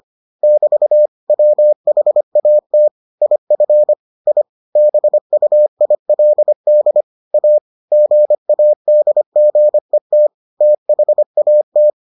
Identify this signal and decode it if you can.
Morse code, heard as a keyed tone, about 25 wpm, 600 Hz. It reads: <BT> WHAT IF I BUILD A GADGET THAT